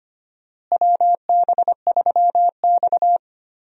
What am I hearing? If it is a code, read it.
Morse code, heard as a keyed tone, about 25 wpm, 700 Hz. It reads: WB3X
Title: USA Call Signs - 4 Characters - Repeated 3x - 25wpm